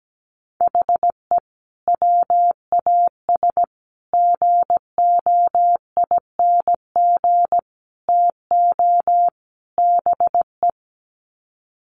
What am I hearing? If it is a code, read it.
Morse code, heard as a keyed tone, about 17 wpm, 700 Hz. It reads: HE WAS GOING TO BE
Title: Sets of 5 Words 17wpm